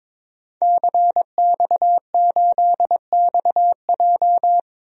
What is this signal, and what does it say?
Morse code, heard as a keyed tone, about 21 wpm, 700 Hz. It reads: CX8XJ
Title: DXCC-100 Call Signs - 5 Characters - 22wpm